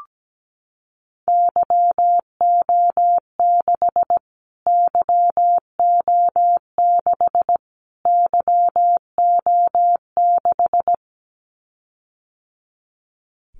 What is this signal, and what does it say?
Morse code, heard as a keyed tone, about 17 wpm, 700 Hz. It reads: YO6 YO6 YO6
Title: Callsign Prefixes for Top 25 DXCC - Repeated 3x - 17wpm